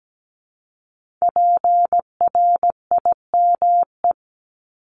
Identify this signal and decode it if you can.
Morse code, heard as a keyed tone, about 17 wpm, 700 Hz. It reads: PRIME
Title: Multi-letter Phonemes Repeated 3x 17wpm